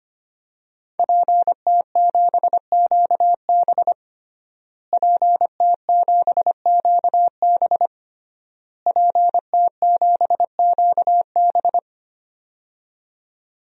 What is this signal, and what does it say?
Morse code, heard as a keyed tone, about 25 wpm, 700 Hz. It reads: PT7QB PT7QB PT7QB
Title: DXCC-100 Call Signs - 5 Characters - Repeated 3x - 25wpm